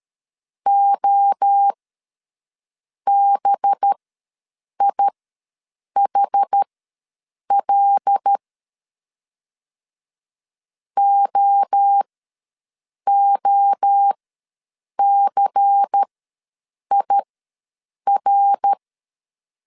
Morse code, heard as a keyed tone, about 13 wpm, spaced as for 7 wpm, 800 Hz. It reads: OBIHL OOCIR
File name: LEZIONI / 5 LBVC / 5 LBVC ws 12wpm 3159